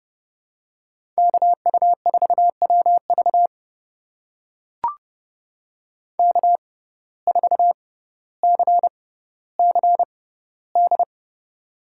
Morse code, heard as a keyed tone, about 30 wpm, spaced as for 15 wpm, 700 Hz. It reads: KU4WV K4CCD
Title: USA Call Signs - 5 Characters - 6x Character Spacing - 30wpm